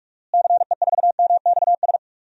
Morse code, heard as a keyed tone, about 44 wpm, 700 Hz. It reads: CE4MXS
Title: DXCC-100 Call Signs - 6 Characters - Repeated 3x - 45wpm